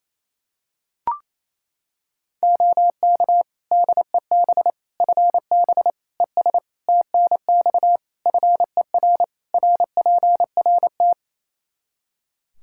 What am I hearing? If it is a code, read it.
Morse code, heard as a keyed tone, about 27 wpm, 700 Hz. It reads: OK DEB FB ES TNX FER RPRT
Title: Practice QSO - Exchange 2 - Info - 28wpm